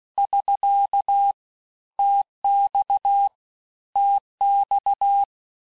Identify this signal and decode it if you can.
Morse code, heard as a keyed tone, about 16 wpm, 800 Hz. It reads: <SK> TX TX
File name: cours-du-lundi files / cours-1592--teste-abreviations-vitesse-15mpm-puis-20mpm